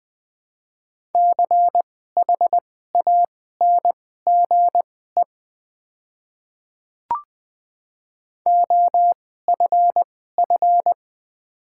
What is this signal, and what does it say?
Morse code, heard as a keyed tone, about 20 wpm, spaced as for 14 wpm, 700 Hz. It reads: CHANGE OFF
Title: 2x Character Spacing - Top 300 Words - 20wpm